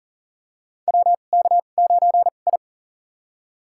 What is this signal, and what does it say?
Morse code, heard as a keyed tone, about 40 wpm, spaced as for 29 wpm, 700 Hz. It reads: WK9I
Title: USA Call Signs - 4 Characters - 2x Character Spacing - 40wpm